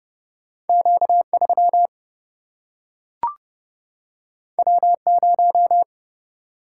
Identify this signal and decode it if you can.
Morse code, heard as a keyed tone, about 30 wpm, 700 Hz. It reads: Q3 W0
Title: Single-Letter Single-Number 30wpm